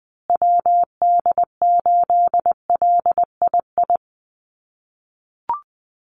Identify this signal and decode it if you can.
Morse code, heard as a keyed tone, about 19 wpm, 700 Hz. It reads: WD8LII
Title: USA Call Signs Repeated 2x 20wpm